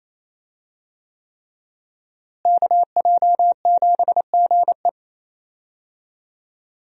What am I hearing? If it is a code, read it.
Morse code, heard as a keyed tone, about 28 wpm, 700 Hz. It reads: KJ7GE
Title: USA Call Signs - 5 Characters - Repeated 3x - 28wpm